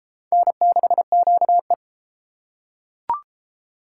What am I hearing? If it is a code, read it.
Morse code, heard as a keyed tone, about 28 wpm, 700 Hz. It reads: N6QE